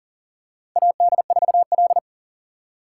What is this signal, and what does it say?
Morse code, heard as a keyed tone, about 40 wpm, 700 Hz. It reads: AD4L
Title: USA Call Signs - 3x Character Spacing - 40wpm